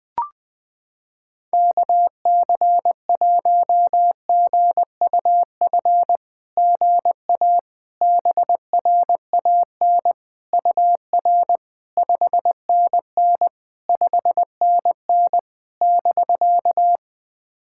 Morse code, heard as a keyed tone, about 20 wpm, 700 Hz. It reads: KC1GUF GA BRAN UR 5NN 5NN <BK>
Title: SOTA QSO Practice 20wpm